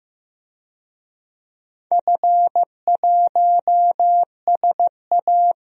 Morse code, heard as a keyed tone, about 15 wpm, 700 Hz. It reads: F1SA
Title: DXCC-100 Call Signs - 4 Characters - Repeated 2x - 15wpm